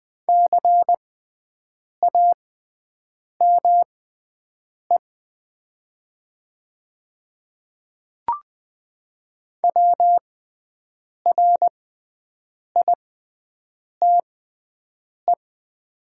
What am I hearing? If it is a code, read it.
Morse code, heard as a keyed tone, about 20 wpm, spaced as for 7 wpm, 700 Hz. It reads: CAME WRITE
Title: 6x Character Spacing - Top 500 Words - 20wpm